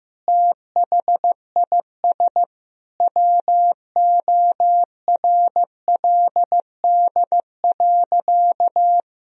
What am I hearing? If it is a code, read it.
Morse code, heard as a keyed tone, about 15 wpm, 700 Hz. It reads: THIS WORLD.
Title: Sentences from Top 1000 Words - Review Part 1 - 15wpm